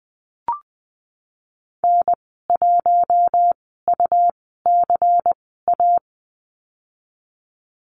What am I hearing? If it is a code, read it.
Morse code, heard as a keyed tone, about 20 wpm, spaced as for 15 wpm, 700 Hz. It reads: N1UCA
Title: 167 Lesson 27g - Number 7 - Call Signs 20wpm